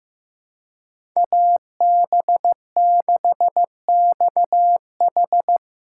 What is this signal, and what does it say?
Morse code, heard as a keyed tone, about 15 wpm, 700 Hz. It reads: AB6XH
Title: USA Call Signs - 3x Character Spacing - 15wpm